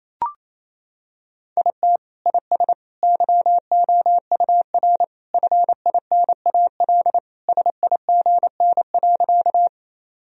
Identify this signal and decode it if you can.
Morse code, heard as a keyed tone, about 28 wpm, 700 Hz. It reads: IT IS YOUR FINAL SIGN.
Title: Sentences from Top 1000 Words - Review Part 2 - 28wpm